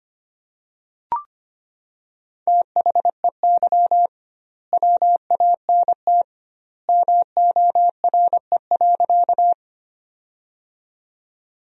Morse code, heard as a keyed tone, about 25 wpm, 700 Hz. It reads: THEY WANT MORE.
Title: 2x Word-Spacing - Sentences from Top 200 Words - 25wpm